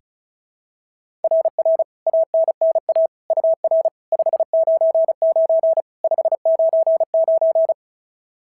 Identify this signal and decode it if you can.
Morse code, heard as a keyed tone, about 36 wpm, 690 Hz. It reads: RR ANNA UR 599 599
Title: SOTA QSO Practice 35wpm